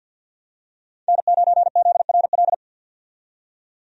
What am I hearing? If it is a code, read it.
Morse code, heard as a keyed tone, about 48 wpm, 700 Hz. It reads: N9ZRL
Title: USA Call Signs - 5 Characters - 50wpm